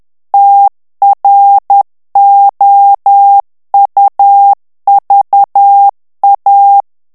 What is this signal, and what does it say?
Morse code, heard as a keyed tone, about 10 wpm, 800 Hz. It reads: TROUVA
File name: morse mp3 / FRANCAIS TEXTE CLAIR 2 VITESSE 10 MOTS MINUTES